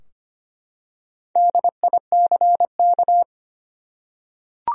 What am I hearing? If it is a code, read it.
Morse code, heard as a keyed tone, about 25 wpm, 700 Hz. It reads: DICK
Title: USA Common Names - Max 5 Letters - 25wpm